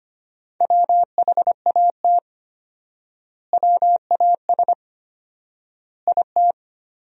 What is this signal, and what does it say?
Morse code, heard as a keyed tone, about 25 wpm, 700 Hz. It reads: WHAT WAS IT
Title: 4x Word-Spacing - Sentences from Top 100 Words - 25wpm